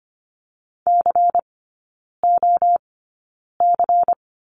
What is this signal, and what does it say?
Morse code, heard as a keyed tone, about 25 wpm, spaced as for 9 wpm, 700 Hz. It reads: COC
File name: course / 25:10wpm / 093 Lesson 16e - Letter c - Three Characters 25wpm